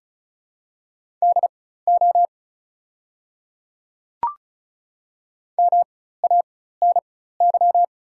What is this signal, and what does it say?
Morse code, heard as a keyed tone, about 36 wpm, spaced as for 16 wpm, 690 Hz. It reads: DO MANY
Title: 4x Character Spacing - Top 200 Words - 35wpm